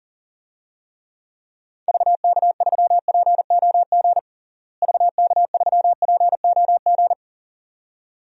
Morse code, heard as a keyed tone, about 37 wpm, 700 Hz. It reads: VK3POG VK3POG
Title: DXCC-100 Call Signs - 6 Characters - Repeated 2x - 40wpm